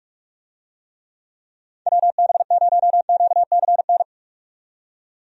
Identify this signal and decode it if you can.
Morse code, heard as a keyed tone, about 46 wpm, 700 Hz. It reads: WB0QCN
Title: USA Call Signs - 6 Characters - 45wpm